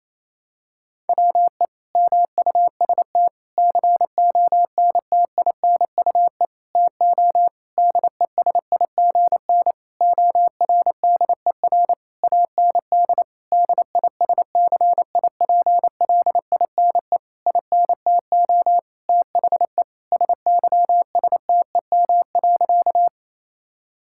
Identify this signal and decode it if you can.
Morse code, heard as a keyed tone, about 28 wpm, 700 Hz. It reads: WE MUST CONTINUE TO DESIGN ORDER AND DISCIPLINE INTO THE SYSTEM.
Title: Sentences from Top 1000 Words - Review Part 6 - 28wpm